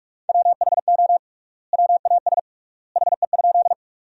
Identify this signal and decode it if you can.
Morse code, heard as a keyed tone, about 46 wpm, 700 Hz. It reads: WHO WAS HE?
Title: 3x Word-Spacing - Sentences from Top 100 Words - 45wpm